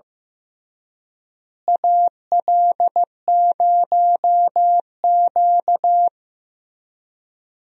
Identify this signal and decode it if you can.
Morse code, heard as a keyed tone, about 15 wpm, 700 Hz. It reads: AL0Q
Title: USA Call Signs - 4 Characters - Repeated 3x - 15wpm